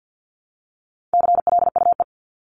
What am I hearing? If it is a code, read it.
Morse code, heard as a keyed tone, about 50 wpm, 700 Hz. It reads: CLUE
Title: Top 1751-2000 Words 50wpm